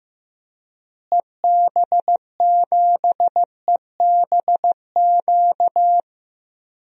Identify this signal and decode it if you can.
Morse code, heard as a keyed tone, about 14 wpm, 700 Hz. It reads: EB7EBQ
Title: DXCC-100 Call Signs - 6 Characters - Repeated 3x - 15wpm